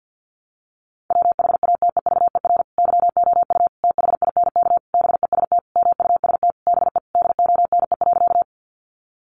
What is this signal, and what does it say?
Morse code, heard as a keyed tone, about 50 wpm, 700 Hz. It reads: WHATEVER YOU THINK BEST MUST BE DONE.
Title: Sentences from Top 1100 Words 50wpm